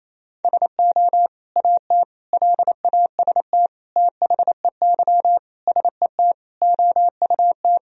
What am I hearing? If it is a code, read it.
Morse code, heard as a keyed tone, about 28 wpm, 700 Hz. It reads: SO AT LAST THEY SET OUT
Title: Sentences from Top 1000 Words - Review Part 1 - 28wpm